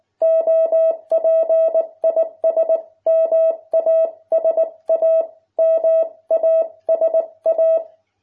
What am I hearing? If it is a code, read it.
Morse code, heard as a keyed tone, about 19 wpm, 660 Hz. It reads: OP IS MASA MASA